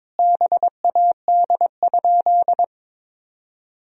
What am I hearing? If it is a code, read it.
Morse code, heard as a keyed tone, about 22 wpm, 700 Hz. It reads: BAD?
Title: Sentences from Top 1000 Words - Review Part 1 - 22wpm